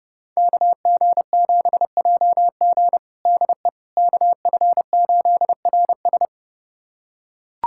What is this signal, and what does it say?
Morse code, heard as a keyed tone, about 32 wpm, 700 Hz. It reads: KG7JG DE KF8RS
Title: USA Callsign DE Callsign - 5 Characters - 30wpm